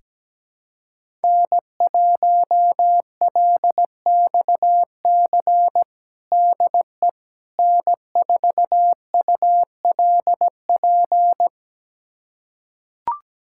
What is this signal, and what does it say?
Morse code, heard as a keyed tone, about 17 wpm, 700 Hz. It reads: N1LXC DE N4ULP
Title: USA Callsign DE Callsign - 5 Characters - 17wpm